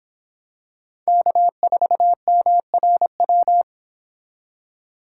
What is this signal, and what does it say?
Morse code, heard as a keyed tone, about 26 wpm, 700 Hz. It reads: K4MRW